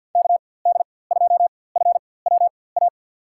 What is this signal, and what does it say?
Morse code, heard as a keyed tone, about 52 wpm, spaced as for 23 wpm, 700 Hz. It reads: KD2FWA
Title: USA Call Signs - 4x Character Spacing - 50wpm